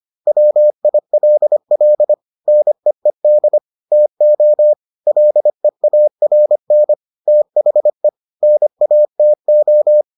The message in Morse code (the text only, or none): WILL NEED TO LEARN THE NATO